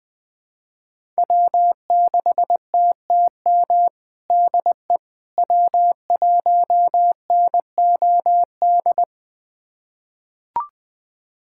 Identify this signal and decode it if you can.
Morse code, heard as a keyed tone, about 20 wpm, 700 Hz. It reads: W6TTM DE W1NOD
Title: USA Callsign DE Callsign - 5 Characters - 20wpm